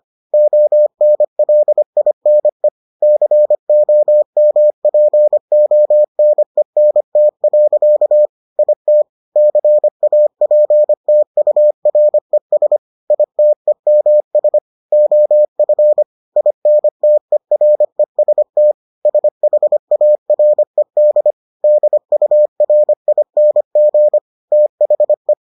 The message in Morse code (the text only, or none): ONLINE COMPONENT. IT CAPTURES ITEMS OF INTEREST SHARED DURING THE